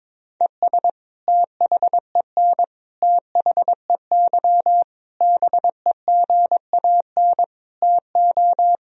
Morse code, heard as a keyed tone, about 22 wpm, 700 Hz. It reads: ES THEN THEY BEGAN TO
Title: Sentences from Top 300 Words and Abbreviations 22wpm